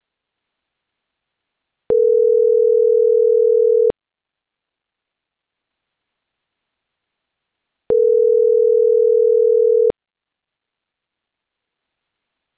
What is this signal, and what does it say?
Ringback tone (North America), 440 + 480 Hz.